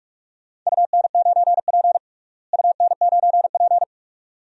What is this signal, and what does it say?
Morse code, heard as a keyed tone, about 46 wpm, 700 Hz. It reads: UN9P UN9P
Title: DXCC-100 Call Signs - 4 Characters - Repeated 2x - 45wpm